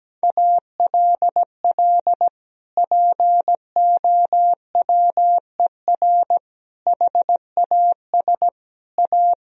Morse code, heard as a keyed tone, about 17 wpm, 700 Hz. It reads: ALL POWER HAS A
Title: Sentences from Top 1000 Words - Review Part 2 - 17wpm